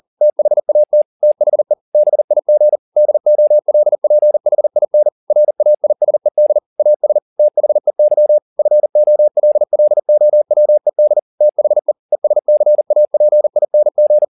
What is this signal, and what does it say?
Morse code, heard as a keyed tone, about 40 wpm, 600 Hz. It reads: THAT THE BIG DOLPHIN RAISED AS THEY FOLLOWED THE ESCAPING